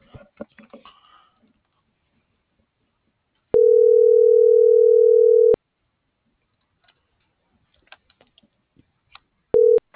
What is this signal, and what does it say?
Ringback tone (North America), 440 + 480 Hz.